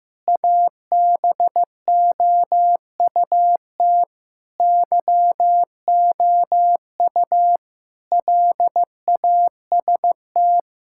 Morse code, heard as a keyed tone, about 15 wpm, 700 Hz. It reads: ABOUT YOU LAST
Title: Sentences from Top 1600 Words 15wpm